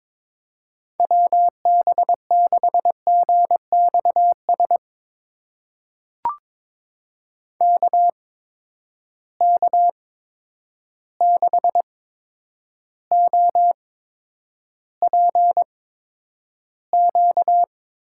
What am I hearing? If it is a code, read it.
Morse code, heard as a keyed tone, about 22 wpm, spaced as for 9 wpm, 700 Hz. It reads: WB6GXS KK6OPQ
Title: USA Call Signs - 6 Characters - 8x Character Spacing - 22wpm